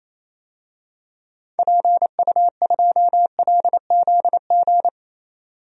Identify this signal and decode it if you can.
Morse code, heard as a keyed tone, about 27 wpm, 700 Hz. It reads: PU2LZG
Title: DXCC-100 Call Signs - 6 Characters - 28wpm